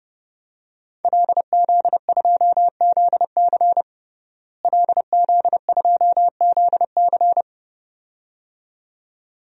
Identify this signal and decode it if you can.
Morse code, heard as a keyed tone, about 29 wpm, 700 Hz. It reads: LZ2ZC LZ2ZC
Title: DXCC-100 Call Signs - 5 Characters - Repeated 2x - 30wpm